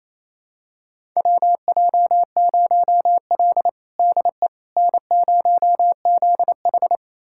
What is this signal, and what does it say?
Morse code, heard as a keyed tone, about 28 wpm, 700 Hz. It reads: WJ0L DE N0ZH
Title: USA Callsign DE Callsign - 4 Characters - 28wpm